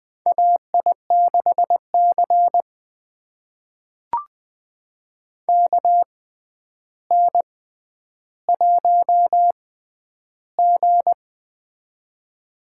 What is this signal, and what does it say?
Morse code, heard as a keyed tone, about 20 wpm, spaced as for 10 wpm, 700 Hz. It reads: AI6C KN1G
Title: USA Call Signs - 4 Characters - 6x Character Spacing - 20wpm